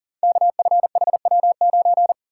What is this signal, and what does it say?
Morse code, heard as a keyed tone, about 38 wpm, 700 Hz. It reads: KFHW9
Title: Code Groups - 5 Letter:Numbers - 40wpm